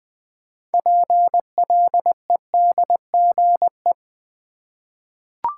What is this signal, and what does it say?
Morse code, heard as a keyed tone, about 20 wpm, 700 Hz. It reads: PLEDGE
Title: 6 Letter Words - First 3 Letters Given - 20wpm